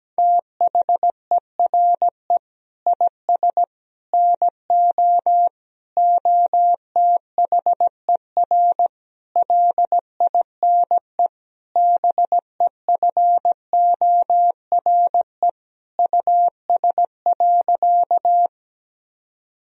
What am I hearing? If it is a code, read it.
Morse code, heard as a keyed tone, about 17 wpm, 700 Hz. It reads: THERE IS NO OTHER LINE BEFORE US.
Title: Sentences from Top 1000 Words - Review Part 1 - 17wpm